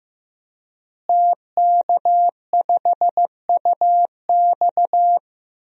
Morse code, heard as a keyed tone, about 15 wpm, 700 Hz. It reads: TK5UX
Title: DXCC-100 Call Signs - 5 Characters - Repeated 3x - 15wpm